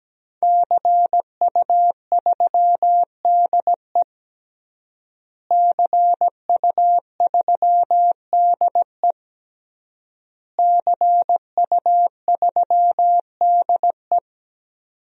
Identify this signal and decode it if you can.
Morse code, heard as a keyed tone, about 17 wpm, 700 Hz. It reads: CU3DE CU3DE CU3DE
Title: DXCC-100 Call Signs - 5 Characters - Repeated 3x - 17wpm